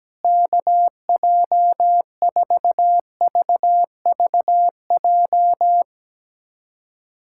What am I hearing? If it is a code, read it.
Morse code, heard as a keyed tone, about 17 wpm, 700 Hz. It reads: KJ4VVJ
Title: USA Callsign DE Callsign 17wpm